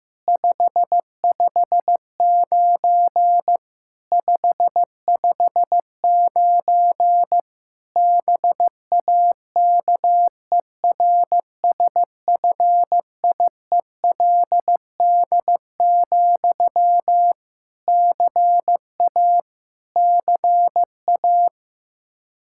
Morse code, heard as a keyed tone, about 15 wpm, 700 Hz. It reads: 559 559 BAKERSFIELD, CA CA
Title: K1USN SST QSO Practice 15wpm